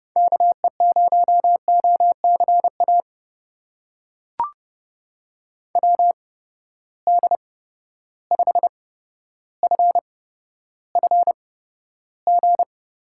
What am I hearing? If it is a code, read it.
Morse code, heard as a keyed tone, about 30 wpm, spaced as for 13 wpm, 700 Hz. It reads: KE0OCA WD5FFG
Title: USA Call Signs - 6 Characters - 8x Character Spacing - 30wpm